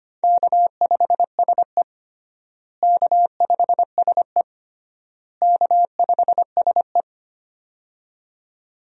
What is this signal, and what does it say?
Morse code, heard as a keyed tone, about 25 wpm, 700 Hz. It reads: K5SE K5SE K5SE
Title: USA Call Signs - 4 Characters - Repeated 3x - 25wpm